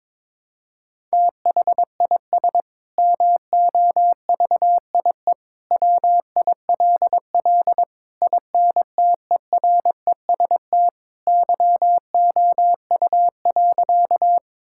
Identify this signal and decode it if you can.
Morse code, heard as a keyed tone, about 22 wpm, 700 Hz. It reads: THIS MOVIE WILL INTEREST YOU.
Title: Sentences from Top 1000 Words - Review Part 5 - 22wpm